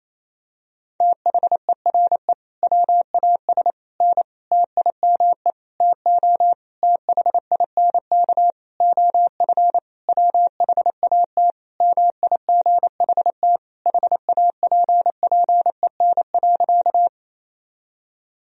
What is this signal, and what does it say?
Morse code, heard as a keyed tone, about 28 wpm, 700 Hz. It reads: THERE WAS N TIME TO THINK OF WHAT MIGHT HAPPEN.
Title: Sentences from Top 400 Words and Abbreviations 28wpm